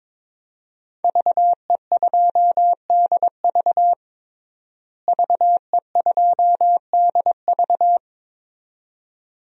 Morse code, heard as a keyed tone, about 22 wpm, 700 Hz. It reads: VE2DV VE2DV
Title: DXCC-100 Call Signs - 5 Characters - Repeated 2x - 22wpm